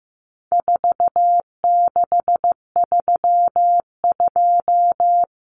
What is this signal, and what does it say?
Morse code, heard as a keyed tone, about 15 wpm, 700 Hz. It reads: 4632
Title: Common Cut Numbers - 4-Digit 15wpm